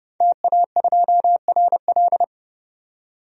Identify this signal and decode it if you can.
Morse code, heard as a keyed tone, about 31 wpm, 700 Hz. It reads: TA2RL
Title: DXCC-100 Call Signs - 5 Characters - 30wpm